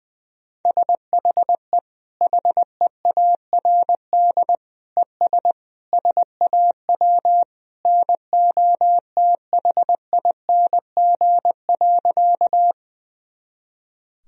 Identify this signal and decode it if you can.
Morse code, heard as a keyed tone, about 21 wpm, 700 Hz. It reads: SHE HEARD ES SAW NOTHING.
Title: Sentences from Top 500 Words and Abbreviations 20wpm